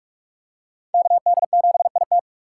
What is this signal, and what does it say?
Morse code, heard as a keyed tone, about 47 wpm, 690 Hz. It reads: KD7IT
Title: USA Call Signs - 5 Characters - 4x Character Spacing - 45wpm